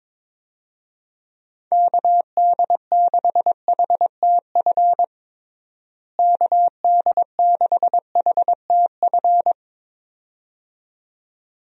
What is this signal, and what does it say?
Morse code, heard as a keyed tone, about 22 wpm, 700 Hz. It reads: KD6HTF KD6HTF
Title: USA Call Signs - 6 Characters - Repeated 2x - 22wpm